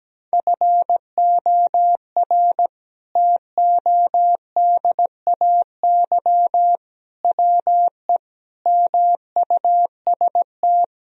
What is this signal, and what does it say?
Morse code, heard as a keyed tone, about 17 wpm, 700 Hz. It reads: FOR TODAY WE MUST
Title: Sentences from Top 1000 Words - Review Part 1 - 17wpm